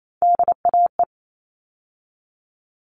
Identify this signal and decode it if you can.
Morse code, heard as a keyed tone, about 28 wpm, 700 Hz. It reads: DAE
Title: All Cut Numbers - 3-Digit 28wpm